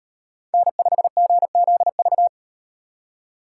Morse code, heard as a keyed tone, about 34 wpm, 700 Hz. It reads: N5GZV